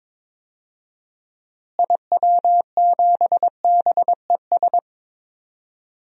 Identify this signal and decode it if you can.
Morse code, heard as a keyed tone, about 21 wpm, 700 Hz. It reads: IW7BES
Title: DXCC-100 Call Signs - 6 Characters - Repeated 3x - 22wpm